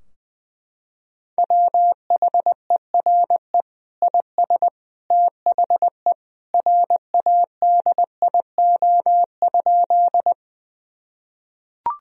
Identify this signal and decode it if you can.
Morse code, heard as a keyed tone, about 21 wpm, 700 Hz. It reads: WHERE IS THE RADIO?
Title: Sentences from Top 1000 Words - Review Part 3 - 20wpm